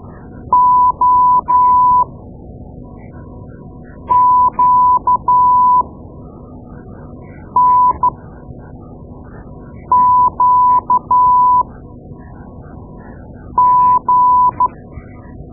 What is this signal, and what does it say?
Morse code, heard as a keyed tone, about 12 wpm, spaced as for 4 wpm, 1000 Hz. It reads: OQNQG
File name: Morse Lessons / Morse Lessons / Tape-2A